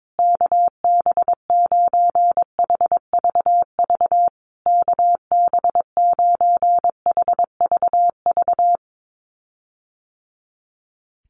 Morse code, heard as a keyed tone, about 22 wpm, 700 Hz. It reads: KB9HVV KB9HVV
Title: USA Call Signs Repeated 3x 22wpm